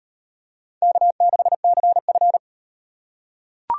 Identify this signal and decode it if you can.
Morse code, heard as a keyed tone, about 37 wpm, 700 Hz. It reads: K6CF